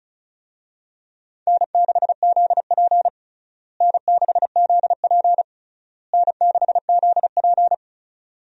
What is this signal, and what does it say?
Morse code, heard as a keyed tone, about 36 wpm, 700 Hz. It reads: N6ZP N6ZP N6ZP
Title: USA Call Signs - 4 Characters - Repeated 3x - 35wpm